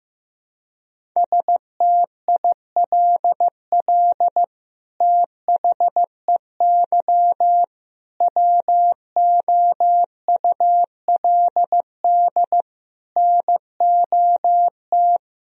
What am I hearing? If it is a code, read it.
Morse code, heard as a keyed tone, about 15 wpm, 700 Hz. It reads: STILL THEY WOULD NOT
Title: Sentences from Top 1000 Words - Review Part 1 - 15wpm